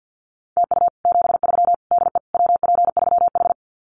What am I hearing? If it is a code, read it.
Morse code, heard as a keyed tone, about 50 wpm, 700 Hz. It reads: TU 73 DE WP3H
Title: Common QSO Phrases Encore 50wpm